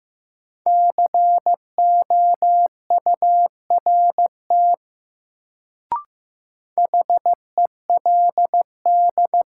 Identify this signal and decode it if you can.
Morse code, heard as a keyed tone, about 15 wpm, 700 Hz. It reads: COURT HELD
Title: Sentences from Top 1600 Words - New Word Drill Alt - 15wpm